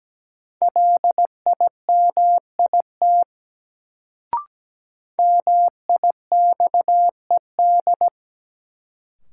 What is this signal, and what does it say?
Morse code, heard as a keyed tone, about 17 wpm, 700 Hz. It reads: LIMIT MIXED
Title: Sentences from Top 1500 Words - New Word Drill - 17wpm